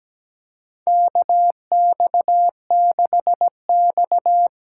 Morse code, heard as a keyed tone, about 17 wpm, 700 Hz. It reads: KX6X
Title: K1USN SST QSO Practice - 17wpm